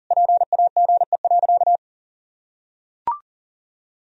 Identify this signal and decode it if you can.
Morse code, heard as a keyed tone, about 40 wpm, 700 Hz. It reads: PAGE.
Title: Sentences from Top 1000 Words - Review Part 1 - 40wpm